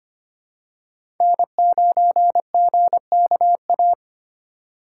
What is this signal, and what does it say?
Morse code, heard as a keyed tone, about 25 wpm, 700 Hz. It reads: N9GKA
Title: USA Call Signs - 4x Character Spacing - 25wpm